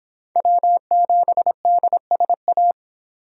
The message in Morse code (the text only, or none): W7DSA